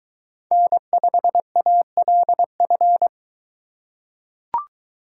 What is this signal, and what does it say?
Morse code, heard as a keyed tone, about 23 wpm, 700 Hz. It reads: N5ALF